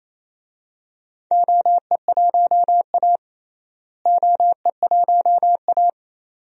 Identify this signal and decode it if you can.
Morse code, heard as a keyed tone, about 28 wpm, 700 Hz. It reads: OE1A OE1A
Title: DXCC-100 Call Signs - 4 Characters - Repeated 2x - 28wpm